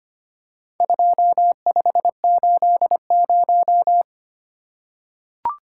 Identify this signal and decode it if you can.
Morse code, heard as a keyed tone, about 25 wpm, 700 Hz. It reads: 2580
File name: Four Letter-Number Combinations - 25wpm